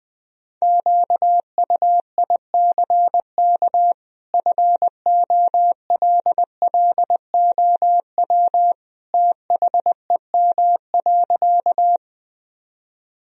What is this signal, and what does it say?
Morse code, heard as a keyed tone, about 20 wpm, 700 Hz. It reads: QUICK FOLLOW THEM.
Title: Sentences from Top 1000 Words - Review Part 1 - 20wpm